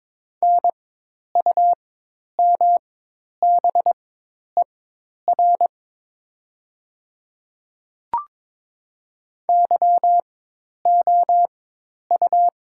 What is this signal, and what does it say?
Morse code, heard as a keyed tone, about 22 wpm, spaced as for 10 wpm, 700 Hz. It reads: NUMBER YOU
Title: 4x Character Spacing - Top 100 Words - 22wpm